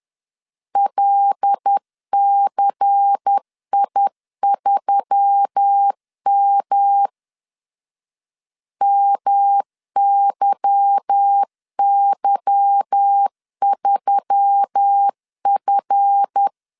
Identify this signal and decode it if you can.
Morse code, heard as a keyed tone, about 10 wpm, 800 Hz. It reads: LCI3M MYY3F
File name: LEZIONI / 8 YJ45 / 8 Y 10wpm 2615